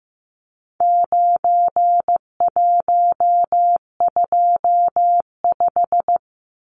Morse code, heard as a keyed tone, about 15 wpm, 700 Hz. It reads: 9125
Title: Numbers - 4-Digit 15wpm